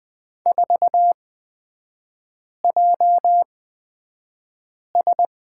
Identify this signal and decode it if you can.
Morse code, heard as a keyed tone, about 20 wpm, spaced as for 5 wpm, 700 Hz. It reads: 4JS
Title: 229 Lesson 37e - Letter x - Three Characters 20wpm